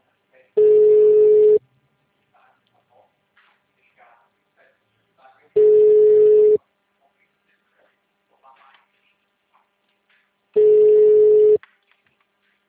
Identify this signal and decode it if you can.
Ringback tone (Europe, 425 Hz), with speech.